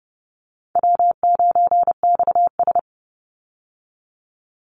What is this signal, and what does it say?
Morse code, heard as a keyed tone, about 30 wpm, 700 Hz. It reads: W9XS
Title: Call Signs Repeated 3x - 30wpm